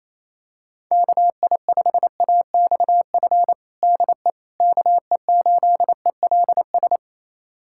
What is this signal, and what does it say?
Morse code, heard as a keyed tone, about 28 wpm, 700 Hz. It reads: KI5AXF DE KE8ELS
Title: USA Callsign DE Callsign - 6 Characters - 28wpm